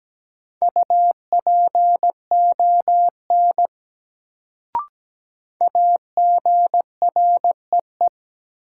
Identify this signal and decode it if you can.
Morse code, heard as a keyed tone, about 17 wpm, 700 Hz. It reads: UPON AGREE
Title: Sentences from Top 1100 Words - New Word Drill Alt - 17wpm